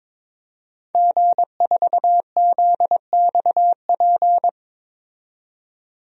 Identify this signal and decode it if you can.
Morse code, heard as a keyed tone, about 22 wpm, 700 Hz. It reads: G4ZXP
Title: DXCC-100 Call Signs - 5 Characters - 22wpm